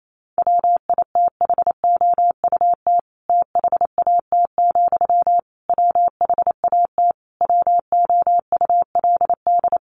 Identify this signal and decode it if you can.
Morse code, heard as a keyed tone, about 28 wpm, 700 Hz. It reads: WITHOUT THAT, WHAT WOULD
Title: Sentences from Top 600 Words 28wpm